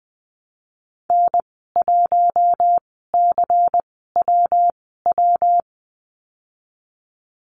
Morse code, heard as a keyed tone, about 20 wpm, spaced as for 14 wpm, 700 Hz. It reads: N1CWW
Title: 140 Lesson 23g - Letter f - Call Signs 20wpm